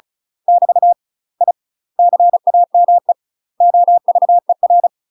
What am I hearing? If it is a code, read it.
Morse code, heard as a keyed tone, about 36 wpm, 700 Hz. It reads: <BT> I CAME OVER
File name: UK MASTER QSOs TRAINING / 35WPM / UK MASTER QSOs 0011